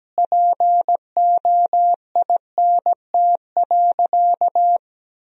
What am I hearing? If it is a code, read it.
Morse code, heard as a keyed tone, about 17 wpm, 700 Hz. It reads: POINT.
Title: 3x Word-Spacing - Sentences from Top 400 Words - 17wpm